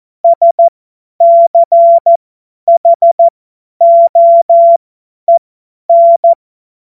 Morse code, heard as a keyed tone, about 14 wpm, spaced as for 10 wpm, 680 Hz. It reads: SCHOEN
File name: Morsekurs / Witze / Witz 18. April 2023 50 F70